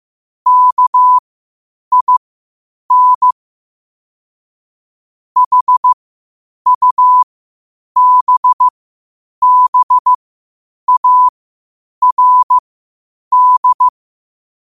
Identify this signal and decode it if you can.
Morse code, heard as a keyed tone, about 15 wpm, spaced as for 8 wpm, 1000 Hz. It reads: KIN HUBBARD
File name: qotdfiles / 8wpm